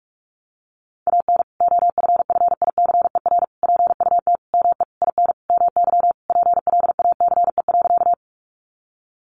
Morse code, heard as a keyed tone, about 44 wpm, 700 Hz. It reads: AN OFFICER PUT ME IN MY PLACE.
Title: Sentences from Top 1200 Words 45wpm